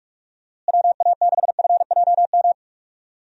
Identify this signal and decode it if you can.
Morse code, heard as a keyed tone, about 45 wpm, 700 Hz. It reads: WA6FJM
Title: USA Call Signs - 6 Characters - Repeated 2x - 45wpm